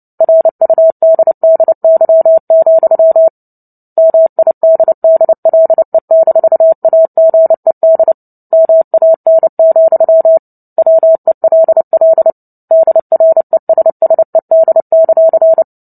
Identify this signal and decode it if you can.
Morse code, heard as a keyed tone, about 30 wpm, 650 Hz. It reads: RUDDY, MIDDLE-AGED MAN, WELL DRESSED;
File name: Book Two / 03-Chapter 3 - 29 WPM